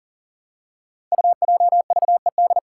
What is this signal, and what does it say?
Morse code, heard as a keyed tone, about 40 wpm, 700 Hz. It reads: UJVED
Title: Five Challenging Letter-Number Combinations 40wpm